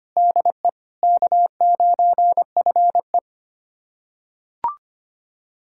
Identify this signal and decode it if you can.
Morse code, heard as a keyed tone, about 25 wpm, 700 Hz. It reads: DE K9FE
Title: USA Callsign DE Callsign - 4 Characters - 25wpm